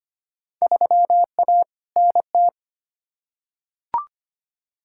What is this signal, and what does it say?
Morse code, heard as a keyed tone, about 25 wpm, 700 Hz. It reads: 3A NT
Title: Field Day Exchanges Encore - 25wpm